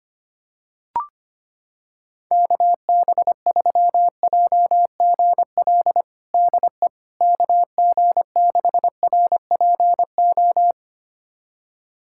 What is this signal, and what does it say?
Morse code, heard as a keyed tone, about 25 wpm, 700 Hz. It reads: KB3JGL DE KG6RPO
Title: USA Callsign DE Callsign 25wpm